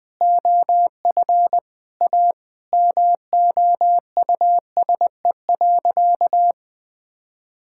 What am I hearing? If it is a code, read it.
Morse code, heard as a keyed tone, about 21 wpm, 700 Hz. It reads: OF A MOUSE.
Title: Sentences from Top 1000 Words - Review Part 5 - 20wpm